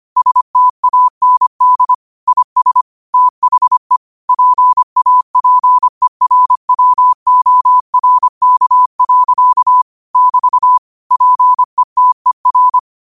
Morse code, heard as a keyed tone, about 25 wpm, 1000 Hz. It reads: STAND IS THE PAPERWORK. <BT> PETER